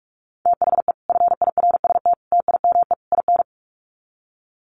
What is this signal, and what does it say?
Morse code, heard as a keyed tone, about 44 wpm, 700 Hz. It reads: THE FIRST TIME IN
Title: Sets of 4 Words 45wpm